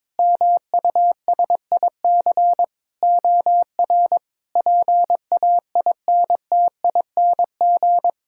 Morse code, heard as a keyed tone, about 22 wpm, 700 Hz. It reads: MUSIC OR PAINTING
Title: Sentences from Top 1000 Words - Review Part 6 - 22wpm